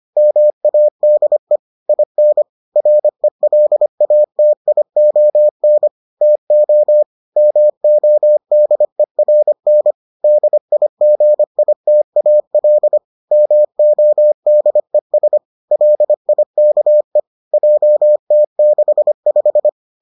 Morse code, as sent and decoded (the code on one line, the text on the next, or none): -- .- -.. / .. -. / .-. .-.. .- - .. --- -. / - --- / -- --- -.. .-. -. / -.. .. --. .. - .- .-.. / -- --- -.. ... / .-.. .. -.- . / .--- - -.... .....
MADE IN RELATION TO MODERN DIGITAL MODES LIKE JT65